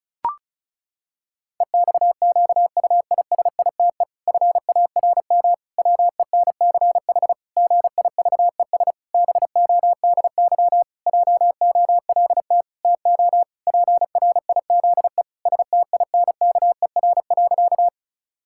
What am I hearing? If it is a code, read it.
Morse code, heard as a keyed tone, about 36 wpm, 700 Hz. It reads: EXQUISITE FARM WENCH GIVES BODY JOLT TO PRIZE STINKER.
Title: Pangram Challenge v2 - 35wpm